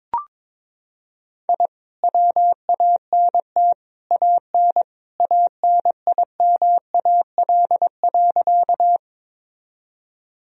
Morse code, heard as a keyed tone, about 22 wpm, 700 Hz. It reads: I WANT AN ANIMAL.
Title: Sentences from Top 1000 Words - Review Part 1 - 22wpm